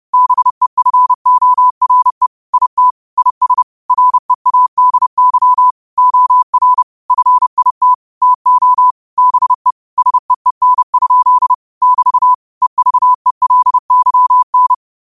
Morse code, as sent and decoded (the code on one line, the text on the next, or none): -... ..-. --- .-. / .. - / .. ... / .-. .- -.. -.-- / --- .-. / ..-. .. - / - --- / -... / ... . -. ..--.. / -...- / . ...- . .-.. -.-- -.
BEFORE IT IS READY OR FIT TO BE SEEN? <BT> EVELYN